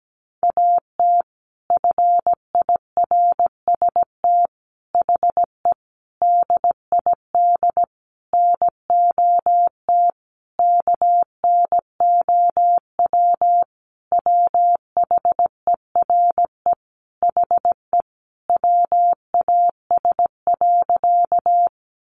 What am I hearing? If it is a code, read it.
Morse code, heard as a keyed tone, about 17 wpm, 700 Hz. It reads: AT FIRST HE DID NOT KNOW WHERE HE WAS.
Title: Sentences from Top 600 Words 17wpm